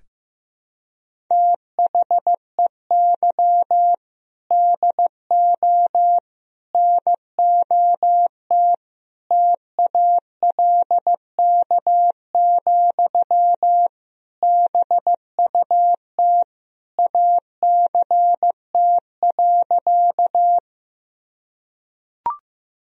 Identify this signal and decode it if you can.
Morse code, heard as a keyed tone, about 15 wpm, 700 Hz. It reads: THEY DO NOT TALK, BUT ACT.
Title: Sentences from Top 1000 Words - Review Part 1 - 15wpm